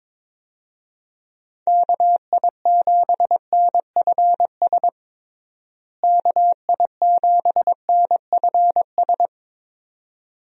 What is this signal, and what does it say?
Morse code, heard as a keyed tone, about 22 wpm, 700 Hz. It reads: KI7NFS KI7NFS
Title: USA Call Signs - 6 Characters - Repeated 2x - 22wpm